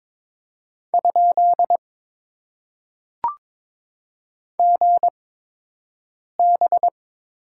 Morse code, heard as a keyed tone, about 22 wpm, spaced as for 6 wpm, 700 Hz. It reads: ? GB
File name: Top 90 QSO Elements - 8x Character Spacing - 22wpm